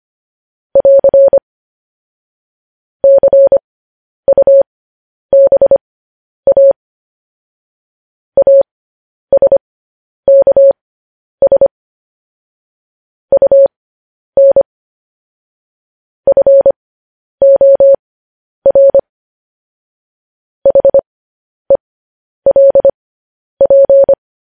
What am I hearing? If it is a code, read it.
Morse code, heard as a keyed tone, about 25 wpm, spaced as for 10 wpm, 550 Hz. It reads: <AR> CUBA ASKS UN FOR HELP